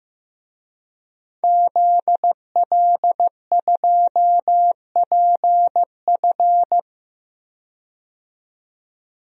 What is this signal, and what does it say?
Morse code, heard as a keyed tone, about 15 wpm, 700 Hz. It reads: ZL2PF
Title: DXCC-100 Call Signs - 5 Characters - 15wpm